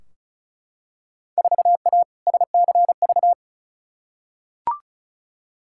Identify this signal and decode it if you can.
Morse code, heard as a keyed tone, about 36 wpm, 700 Hz. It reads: 4A SCV
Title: Field Day Exchanges Encore - 35wpm